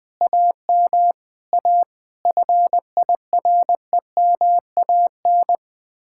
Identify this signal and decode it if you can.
Morse code, heard as a keyed tone, about 20 wpm, 700 Hz. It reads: AM A FIREMAN
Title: Speed-Racing Common QSO Phrases - 20wpm